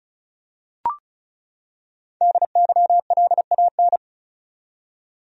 Morse code, heard as a keyed tone, about 36 wpm, 700 Hz. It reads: DYLAN
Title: USA Common Names - Max 5 Letters - 35wpm